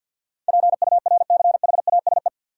Morse code, heard as a keyed tone, about 48 wpm, 700 Hz. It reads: PURCHASE
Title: Sentences from Top 800 Words - New Word Drill - 50wpm